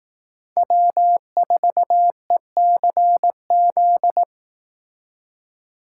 Morse code, heard as a keyed tone, about 18 wpm, 700 Hz. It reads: W4ECZ
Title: Speed Racing - USA 1x3 Call Signs 15wpm